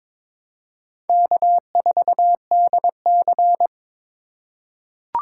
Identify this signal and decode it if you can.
Morse code, heard as a keyed tone, about 22 wpm, 700 Hz. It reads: K4DC
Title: USA Call Signs - 8x Character Spacing - 22wpm